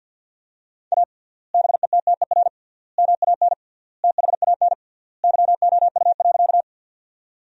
Morse code, heard as a keyed tone, about 50 wpm, 700 Hz. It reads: A BETTER MAN THAN YOU.
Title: 3x Word-Spacing - Sentences from Top 400 Words - 50wpm